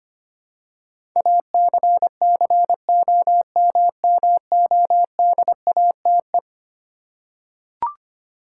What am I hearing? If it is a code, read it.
Morse code, heard as a keyed tone, about 25 wpm, 700 Hz. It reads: ACCOMMODATE
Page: morse-code-ninja-practice-25wpm